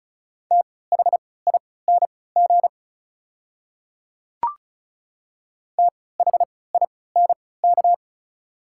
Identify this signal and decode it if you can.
Morse code, heard as a keyed tone, about 36 wpm, spaced as for 20 wpm, 690 Hz. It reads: THING THINK
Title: 3x Character Spacing - Top 500 Words - 35wpm